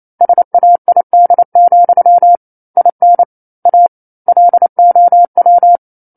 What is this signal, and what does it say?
Morse code, heard as a keyed tone, about 28 wpm, 700 Hz. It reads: SAID, IN A LOW
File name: 09-Chapter-09 28 WPM